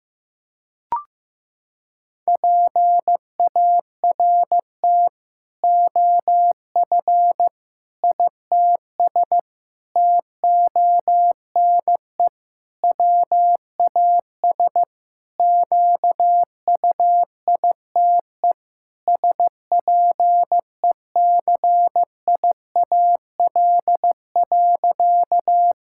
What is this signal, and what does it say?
Morse code, heard as a keyed tone, about 15 wpm, 700 Hz. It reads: PART OF ITS TONE WAS QUITE SPECIAL.
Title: Sentences from Top 1600 Words 15wpm